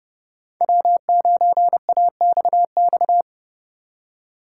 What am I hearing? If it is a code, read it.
Morse code, heard as a keyed tone, about 27 wpm, 700 Hz. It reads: W9AXX